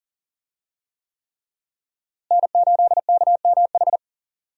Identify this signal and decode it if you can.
Morse code, heard as a keyed tone, about 40 wpm, 700 Hz. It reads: N8KMH
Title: USA Call Signs - 5 Characters - Repeated 3x - 40wpm